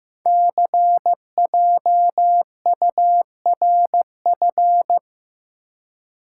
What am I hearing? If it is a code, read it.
Morse code, heard as a keyed tone, about 15 wpm, 700 Hz. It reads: CJURF
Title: Code Groups - 5 Letters - 15wpm